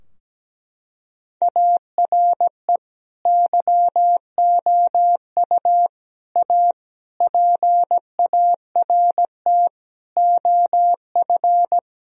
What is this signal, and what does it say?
Morse code, heard as a keyed tone, about 17 wpm, 700 Hz. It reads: ARE YOU A PART OF
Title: Sentences from Top 1000 Words - Review Part 4 - 17wpm